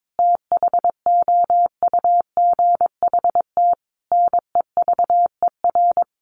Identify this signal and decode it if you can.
Morse code, heard as a keyed tone, about 22 wpm, 700 Hz. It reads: THOUGHT NEVER
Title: Sentences from Top 900 Words 22wpm